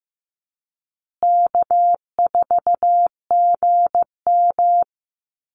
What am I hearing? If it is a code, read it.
Morse code, heard as a keyed tone, about 15 wpm, 700 Hz. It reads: K4GM
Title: Easy Call Signs Repeated 3x - 15wpm